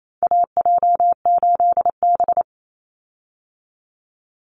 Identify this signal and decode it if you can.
Morse code, heard as a keyed tone, about 28 wpm, 700 Hz. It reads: AJ8B
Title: Call Signs Repeated 2x - 28wpm